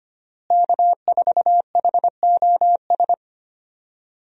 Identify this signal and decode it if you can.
Morse code, heard as a keyed tone, about 25 wpm, 700 Hz. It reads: K4HOS
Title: USA Call Signs - 5 Characters - Repeated 3x - 25wpm